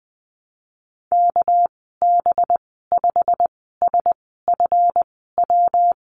Morse code, heard as a keyed tone, about 20 wpm, spaced as for 15 wpm, 700 Hz. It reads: KB5SFW
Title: 238 Lesson 38g - Letter q - Call Signs 20wpm